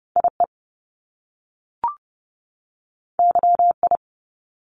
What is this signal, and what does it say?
Morse code, heard as a keyed tone, about 30 wpm, 700 Hz. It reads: IE YI
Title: Top Two Letter Combinations in Words 30wpm